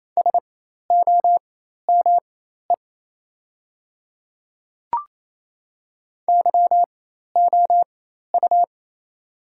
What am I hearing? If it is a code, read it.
Morse code, heard as a keyed tone, about 28 wpm, spaced as for 13 wpm, 700 Hz. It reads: SOME YOU
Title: 4x Character Spacing - Top 200 Words - 28wpm